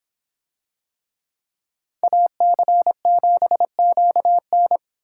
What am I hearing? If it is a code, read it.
Morse code, heard as a keyed tone, about 25 wpm, 700 Hz. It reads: AC7QN